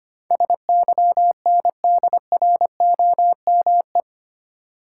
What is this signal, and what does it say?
Morse code, heard as a keyed tone, about 25 wpm, 690 Hz. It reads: SYNDROME
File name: 8 Letter Words - First 3 Letters Given - 25wpm